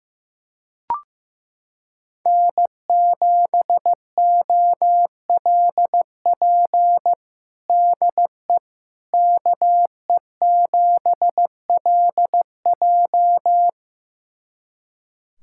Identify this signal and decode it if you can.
Morse code, heard as a keyed tone, about 15 wpm, 700 Hz. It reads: N7OLP DE KE7LJ
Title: USA Callsign DE Callsign - 5 Characters - 15wpm